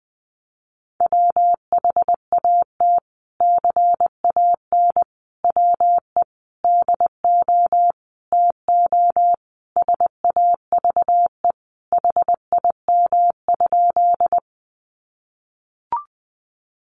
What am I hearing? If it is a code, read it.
Morse code, heard as a keyed tone, about 20 wpm, 700 Hz. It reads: WHAT CAN WE DO TO SAVE HIM?
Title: Sentences from Top 1100 Words 20wpm